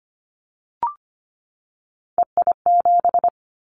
Morse code, heard as a keyed tone, about 25 wpm, 700 Hz. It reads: EI7
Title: Callsign Prefixes for Top 50 DXCC - 25wpm